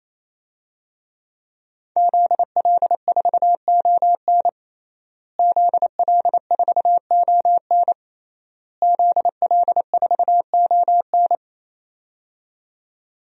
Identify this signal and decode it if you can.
Morse code, heard as a keyed tone, about 27 wpm, 700 Hz. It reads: ZL4ON ZL4ON ZL4ON
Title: DXCC-100 Call Signs - 5 Characters - Repeated 3x - 28wpm